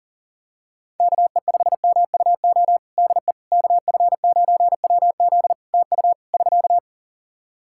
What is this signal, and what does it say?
Morse code, heard as a keyed tone, about 40 wpm, 700 Hz. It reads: KE5MUO DE KF9WZ TU <SK>
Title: K1USN SST QSO Practice 40wpm